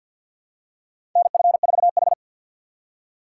Morse code, heard as a keyed tone, about 52 wpm, 700 Hz. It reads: NF4H
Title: USA Call Signs - 4 Characters - 2x Character Spacing - 50wpm